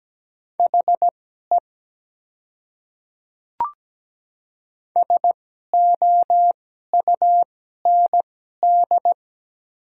Morse code, heard as a keyed tone, about 17 wpm, spaced as for 12 wpm, 700 Hz. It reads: HE SOUND
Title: 2x Character Spacing - Top 100 Words - 17wpm